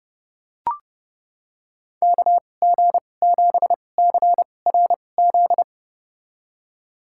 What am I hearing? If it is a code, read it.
Morse code, heard as a keyed tone, about 31 wpm, spaced as for 22 wpm, 700 Hz. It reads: KG7CRZ
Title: USA Call Signs - 6 Characters - 2x Character Spacing - 30wpm